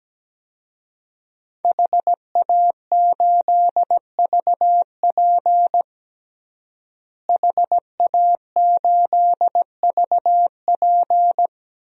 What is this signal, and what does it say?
Morse code, heard as a keyed tone, about 17 wpm, 700 Hz. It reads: HA8VP HA8VP
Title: DXCC-100 Call Signs - 5 Characters - Repeated 3x - 17wpm